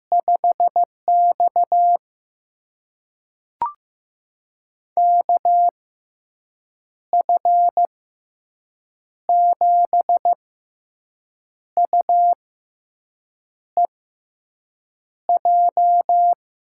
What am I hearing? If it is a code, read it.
Morse code, heard as a keyed tone, about 15 wpm, spaced as for 6 wpm, 700 Hz. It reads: 5X KF7UEJ
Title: USA Call Signs - 6x Character Spacing - 15wpm